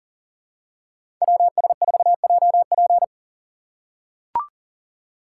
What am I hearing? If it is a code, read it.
Morse code, heard as a keyed tone, about 40 wpm, 700 Hz. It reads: WS4JP
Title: USA Call Signs - 5 Characters - 4x Character Spacing - 40wpm